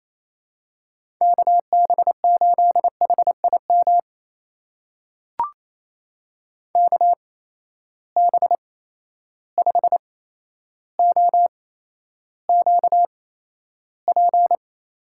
Morse code, heard as a keyed tone, about 28 wpm, spaced as for 12 wpm, 700 Hz. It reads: KB8HIM KB5OQP
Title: USA Call Signs - 6 Characters - 8x Character Spacing - 28wpm